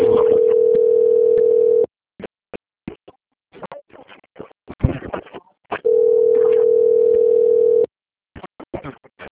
Ringback tone (North America), 440 + 480 Hz.
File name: out- - - - - .12442